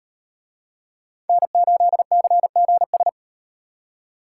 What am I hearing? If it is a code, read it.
Morse code, heard as a keyed tone, about 38 wpm, 700 Hz. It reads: N8CGS